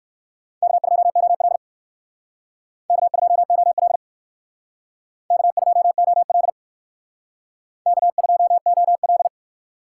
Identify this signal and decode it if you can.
Morse code, heard as a keyed tone, about 58 wpm, 700 Hz. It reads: K2OL K2OL K2OL K2OL